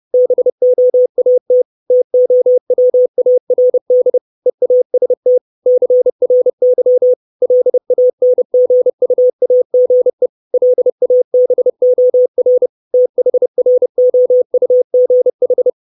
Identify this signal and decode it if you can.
Morse code, heard as a keyed tone, about 30 wpm, 500 Hz. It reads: BOAT TOWARD EAST CRY LANGUAGE LABOR THROUGH
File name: 632 words / 632 Words 30WPM